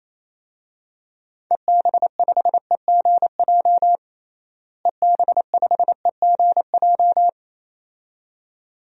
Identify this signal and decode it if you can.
Morse code, heard as a keyed tone, about 28 wpm, 700 Hz. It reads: EB5EGJ EB5EGJ
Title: DXCC-100 Call Signs - 6 Characters - Repeated 2x - 28wpm